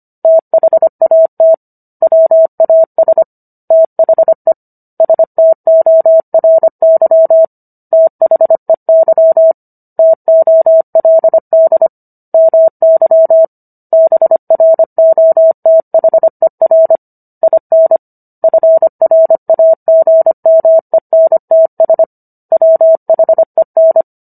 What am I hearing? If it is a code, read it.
Morse code, heard as a keyed tone, about 25 wpm, 650 Hz. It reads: THAT WAS THE STORY THEY TOLD MY BROTHER IN FRAGMENTS WHEN